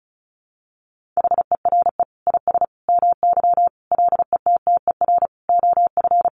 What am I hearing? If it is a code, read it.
Morse code, heard as a keyed tone, about 34 wpm, 700 Hz. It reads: HERE IS MY LETTER OF
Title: Sentences from Top 1400 Words 35wpm